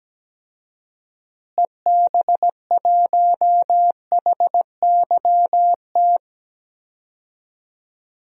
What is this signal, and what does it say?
Morse code, heard as a keyed tone, about 17 wpm, 700 Hz. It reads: EB1HYT
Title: DXCC-100 Call Signs - 6 Characters - Repeated 3x - 17wpm